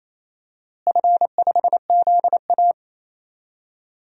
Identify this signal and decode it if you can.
Morse code, heard as a keyed tone, about 27 wpm, 700 Hz. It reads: F5ZA
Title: DXCC-100 Call Signs - 4 Characters - Repeated 3x - 28wpm